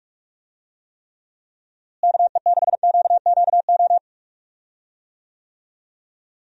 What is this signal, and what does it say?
Morse code, heard as a keyed tone, about 43 wpm, 700 Hz. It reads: KE6QQO
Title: USA Call Signs - 6 Characters - Repeated 3x - 45wpm